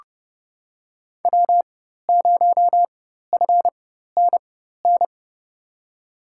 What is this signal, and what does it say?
Morse code, heard as a keyed tone, about 30 wpm, spaced as for 14 wpm, 700 Hz. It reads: W0FNN
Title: USA Call Signs - 5 Characters - 4x Character Spacing - 30wpm